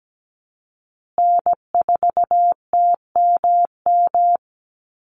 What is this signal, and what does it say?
Morse code, heard as a keyed tone, about 17 wpm, 700 Hz. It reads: N4TMM
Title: Call Signs Repeated 3x - 17wpm